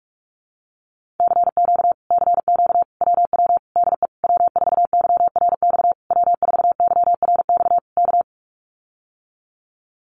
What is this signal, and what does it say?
Morse code, heard as a keyed tone, about 44 wpm, 700 Hz. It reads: CQ CQ WW DE W4YRX W4YRX K
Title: Common QSO Phrases Encore 45wpm